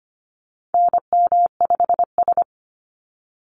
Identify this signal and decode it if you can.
Morse code, heard as a keyed tone, about 25 wpm, 700 Hz. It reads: NM5S
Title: NA SOTA Activator Callsigns 25wpm